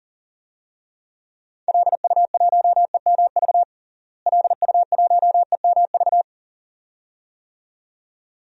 Morse code, heard as a keyed tone, about 40 wpm, 700 Hz. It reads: LU1EMV LU1EMV
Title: DXCC-100 Call Signs - 6 Characters - Repeated 2x - 40wpm